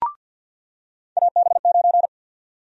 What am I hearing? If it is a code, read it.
Morse code, heard as a keyed tone, about 47 wpm, 700 Hz. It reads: AB9
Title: USA Call Sign Prefixes - 50wpm